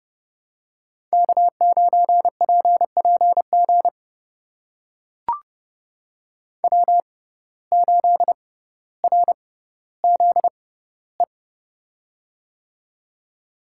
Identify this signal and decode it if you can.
Morse code, heard as a keyed tone, about 31 wpm, spaced as for 15 wpm, 700 Hz. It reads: K9PPG W8RZE
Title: USA Call Signs - 5 Characters - 6x Character Spacing - 30wpm